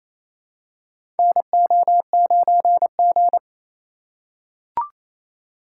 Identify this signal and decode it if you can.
Morse code, heard as a keyed tone, about 28 wpm, 700 Hz. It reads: NO9G
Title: USA Call Signs - 4 Characters - 2x Character Spacing - 28wpm